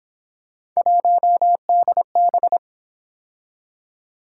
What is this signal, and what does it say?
Morse code, heard as a keyed tone, about 26 wpm, 700 Hz. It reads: 1DB
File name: Speed-Racing - Three Letter-Number Combinations - 22wpm